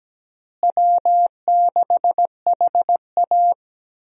Morse code, heard as a keyed tone, about 17 wpm, 700 Hz. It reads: W6HA
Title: USA Call Signs - 4 Characters - Repeated 3x - 17wpm